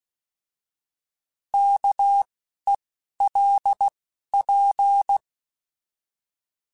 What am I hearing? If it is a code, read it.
Morse code, heard as a keyed tone, about 16 wpm, 800 Hz. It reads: K E L P
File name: Alphabetical / Alphabetical mp3s / Ks @10